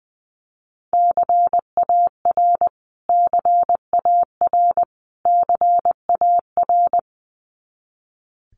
Morse code, heard as a keyed tone, about 20 wpm, 700 Hz. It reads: CAR CAR CAR
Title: Top 300 Words Repeated 20wpm